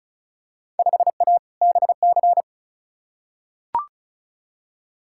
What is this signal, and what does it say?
Morse code, heard as a keyed tone, about 34 wpm, 700 Hz. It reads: 5A BC
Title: Field Day Exchanges - 35wpm